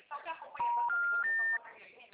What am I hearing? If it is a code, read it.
Special information tone, with speech.